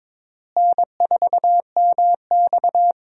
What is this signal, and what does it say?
Morse code, heard as a keyed tone, about 22 wpm, 700 Hz. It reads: N4MX
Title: USA Call Signs - 4 Characters - Repeated 3x - 22wpm